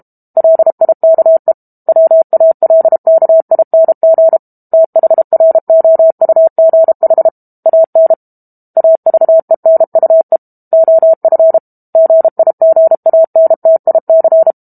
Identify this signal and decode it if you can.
Morse code, heard as a keyed tone, about 32 wpm, 650 Hz. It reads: LIKE WALKING THROUGH AN AVENUE OF GIGANTIC
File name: Chapter 6 - 32 WPM